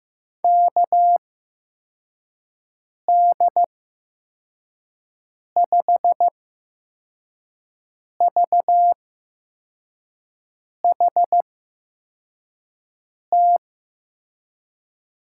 Morse code, heard as a keyed tone, about 15 wpm, spaced as for 4 wpm, 700 Hz. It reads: KD5VHT
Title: USA Call Signs - 6 Characters - 8x Character Spacing - 15wpm